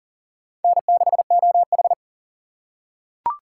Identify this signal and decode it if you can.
Morse code, heard as a keyed tone, about 40 wpm, 700 Hz. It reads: N6OH
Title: USA Call Signs - 4 Characters - 2x Character Spacing - 40wpm